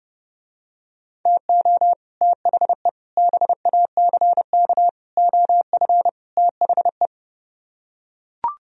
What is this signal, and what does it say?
Morse code, heard as a keyed tone, about 30 wpm, 700 Hz. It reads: TO THE BACK OF THE